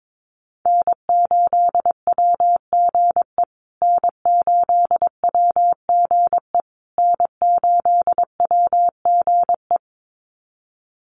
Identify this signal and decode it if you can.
Morse code, heard as a keyed tone, about 22 wpm, 700 Hz. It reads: N8WGE N8WGE N8WGE
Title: USA Call Signs Repeated 3x 22wpm